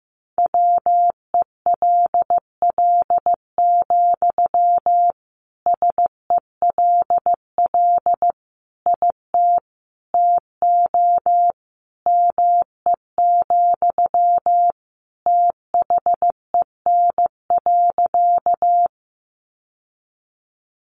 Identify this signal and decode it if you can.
Morse code, heard as a keyed tone, about 15 wpm, 700 Hz. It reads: WELL, SELL IT TO ME, THEN.
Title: Sentences from Top 1100 Words 15wpm